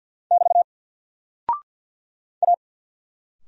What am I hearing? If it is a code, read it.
Morse code, heard as a keyed tone, about 50 wpm, 700 Hz. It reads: <BT> A
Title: Single Character (Letter:Number:Punctuation:Prosign) 50wpm